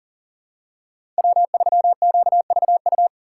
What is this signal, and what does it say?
Morse code, heard as a keyed tone, about 42 wpm, 700 Hz. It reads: W3QVU
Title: USA Call Signs - 5 Characters - 6x Character Spacing - 40wpm